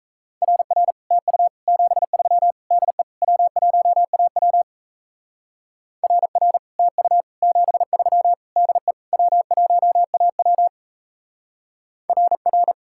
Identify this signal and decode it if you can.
Morse code, heard as a keyed tone, about 34 wpm, 700 Hz. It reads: RR TU 73 DE W1AW RR TU 73 DE W1AW RR
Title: Speed-Racing Common QSO Phrases Encore - 28wpm